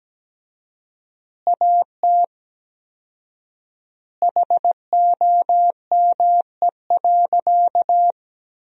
Morse code, heard as a keyed tone, about 17 wpm, 700 Hz. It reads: AT HOME.
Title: 4x Word-Spacing - Sentences from Top 400 Words - 17wpm